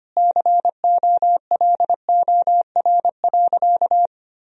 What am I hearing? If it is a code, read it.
Morse code, heard as a keyed tone, about 25 wpm, 700 Hz. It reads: COLOR.
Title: Sentences from Top 1000 Words - Review Part 2 - 25wpm